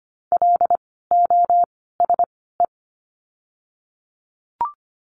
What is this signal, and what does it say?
Morse code, heard as a keyed tone, about 25 wpm, 700 Hz. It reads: L O S E
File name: course / 25:10wpm / 094 Lesson 16f - Letter c - Words 25wpm